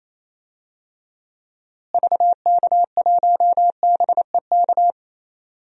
Morse code, heard as a keyed tone, about 28 wpm, 700 Hz. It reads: VK1BEK
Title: DXCC-100 Call Signs - 6 Characters - Repeated 3x - 28wpm